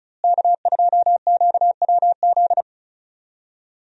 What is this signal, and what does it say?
Morse code, heard as a keyed tone, about 36 wpm, 700 Hz. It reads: K2QWZ
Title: USA Call Signs - 5 Characters - Repeated 2x - 35wpm